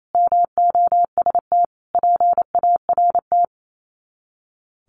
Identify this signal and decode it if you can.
Morse code, heard as a keyed tone, about 28 wpm, 690 Hz. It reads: MOST PART
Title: Sets of 5 Words Encore 28wpm